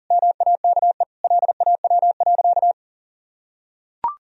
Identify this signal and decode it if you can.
Morse code, heard as a keyed tone, about 40 wpm, 700 Hz. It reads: MAKE LAW.
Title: Sentences from Top 1000 Words - Review Part 3 - 40wpm